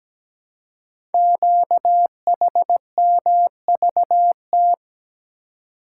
Morse code, heard as a keyed tone, about 17 wpm, 700 Hz. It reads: QHMVT
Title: Code Groups - 5 Letters - 17wpm